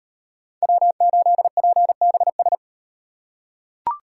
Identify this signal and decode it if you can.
Morse code, heard as a keyed tone, about 38 wpm, 700 Hz. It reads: W8PBS